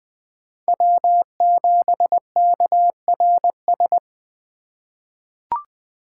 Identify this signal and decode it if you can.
Morse code, heard as a keyed tone, about 20 wpm, 700 Hz. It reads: W7KRS
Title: K1USN SST QSO Practice - 20wpm